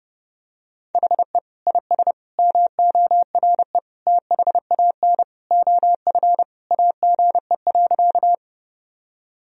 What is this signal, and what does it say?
Morse code, heard as a keyed tone, about 32 wpm, 700 Hz. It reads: HE IS MORE THAN OF AGE.
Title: Sentences from Top 1000 Words - Review Part 3 - 30wpm